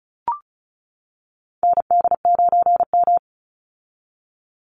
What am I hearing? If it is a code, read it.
Morse code, heard as a keyed tone, about 34 wpm, 700 Hz. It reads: ND9M
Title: Call Signs - 35wpm